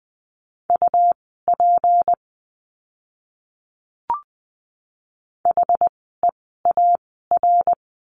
Morse code, heard as a keyed tone, about 20 wpm, spaced as for 15 wpm, 700 Hz. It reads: UP HEAR
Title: 206 Lesson 33f - Letter j - Words 20wpm